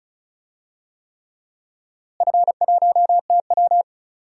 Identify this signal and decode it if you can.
Morse code, heard as a keyed tone, about 34 wpm, 700 Hz. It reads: F1TW
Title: DXCC-100 Call Signs - 4 Characters - Repeated 3x - 35wpm